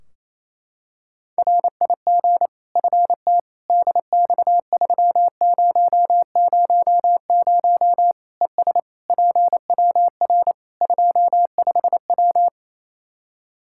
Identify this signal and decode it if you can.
Morse code, heard as a keyed tone, about 28 wpm, 700 Hz. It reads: RIG FT DX3000 ES PWR 25W
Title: K1USN SST QSO Practice 28wpm